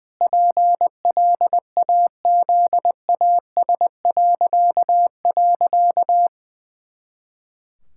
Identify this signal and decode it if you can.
Morse code, heard as a keyed tone, about 20 wpm, 700 Hz. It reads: PLAZAS..
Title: Pangram Challenge v2 - 20wpm